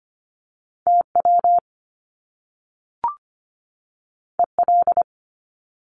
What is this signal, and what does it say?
Morse code, heard as a keyed tone, about 24 wpm, 700 Hz. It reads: TW EL
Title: Top Two Letter Combinations in Words 25wpm